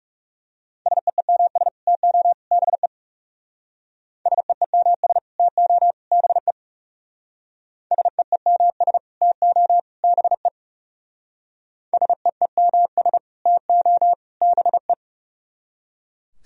Morse code, heard as a keyed tone, about 35 wpm, 700 Hz. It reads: SEEMS TO BE SEEMS TO BE SEEMS TO BE SEEMS TO BE
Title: Speed-Racing - Sets of 3 Words 30wpm